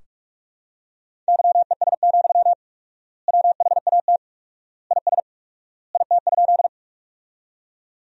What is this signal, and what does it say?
Morse code, heard as a keyed tone, about 47 wpm, 690 Hz. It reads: YES, WHAT IS IT?
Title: 4x Word-Spacing - Sentences from Top 500 Words - 45wpm